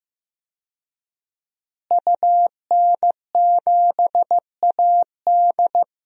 Morse code, heard as a keyed tone, about 15 wpm, 700 Hz. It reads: UN7AD
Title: DXCC-100 Call Signs - 5 Characters - Repeated 3x - 15wpm